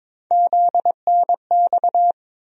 Morse code, heard as a keyed tone, about 22 wpm, 700 Hz. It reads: ZNX
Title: USA Call Sign Postfixes - 3 Letters 22wpm